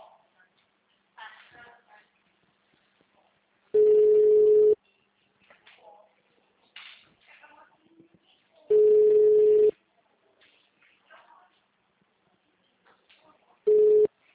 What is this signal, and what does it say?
Ringback tone (Europe, 425 Hz), with speech.